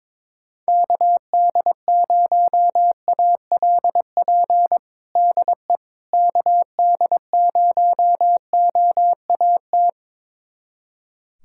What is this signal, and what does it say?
Morse code, heard as a keyed tone, about 22 wpm, 700 Hz. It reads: KD0ALP DE KD0OAT
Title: USA Callsign DE Callsign 22wpm